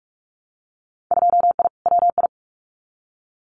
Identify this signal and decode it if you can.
Morse code, heard as a keyed tone, about 44 wpm, 700 Hz. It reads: 2I WI
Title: Winter Field Day Exchanges 45wpm